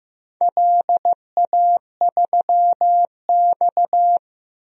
Morse code, heard as a keyed tone, about 15 wpm, 700 Hz. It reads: LA3X
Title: DXCC-100 Call Signs - 4 Characters - Repeated 2x - 15wpm